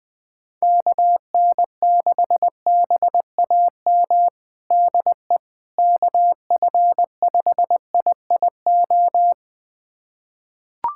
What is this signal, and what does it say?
Morse code, heard as a keyed tone, about 21 wpm, 700 Hz. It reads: KN6BAM DE KF5IIO
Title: USA Callsign DE Callsign - 6 Characters - 20wpm